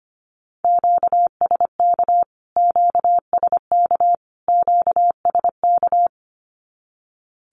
Morse code, heard as a keyed tone, about 25 wpm, 700 Hz. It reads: QSK QSK QSK
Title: Q-Codes Repeated 3x 25wpm